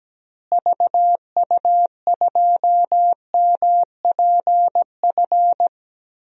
Morse code, heard as a keyed tone, about 17 wpm, 700 Hz. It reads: VU2MPF
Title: DXCC-100 Call Signs - 6 Characters - Repeated 2x - 17wpm